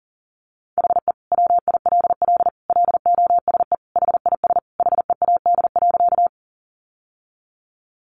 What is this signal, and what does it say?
Morse code, heard as a keyed tone, about 40 wpm, 700 Hz. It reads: HE WILL LOSE HIS HEAD.
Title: Sentences from Top 1100 Words 40wpm